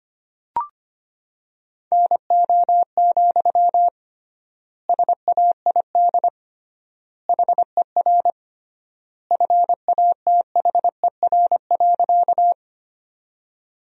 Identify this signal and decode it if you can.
Morse code, heard as a keyed tone, about 25 wpm, 700 Hz. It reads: NO, SAID HER FATHER.
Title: 3x Word-Spacing - Sentences from Top 300 Words - 25wpm